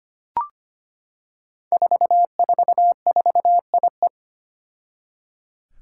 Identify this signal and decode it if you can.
Morse code, heard as a keyed tone, about 25 wpm, 700 Hz. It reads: 444IE
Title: Five Challenging Letter-Number Combinations 25wpm